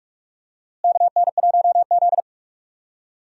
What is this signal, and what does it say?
Morse code, heard as a keyed tone, about 46 wpm, 700 Hz. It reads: KN1Z
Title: USA Call Signs - 4 Characters - Repeated 3x - 45wpm